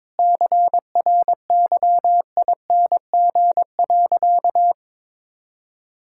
Morse code, heard as a keyed tone, about 22 wpm, 700 Hz. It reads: CRYING.
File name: practice / 22wpm / Sentences with 4-Word Phrases - 22wpm